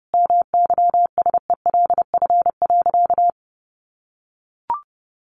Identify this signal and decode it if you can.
Morse code, heard as a keyed tone, about 28 wpm, 700 Hz. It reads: MYSELF.
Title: Sentences from Top 1300 Words 30wpm